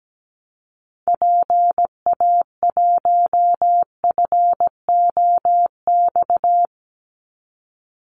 Morse code, heard as a keyed tone, about 17 wpm, 700 Hz. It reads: PA1FOX
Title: Call Signs - 17wpm